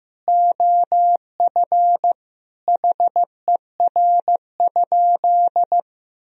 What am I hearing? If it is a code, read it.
Morse code, heard as a keyed tone, about 15 wpm, 700 Hz. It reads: OF HER?
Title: Sentences from Top 1000 Words - Review Part 4 - 15wpm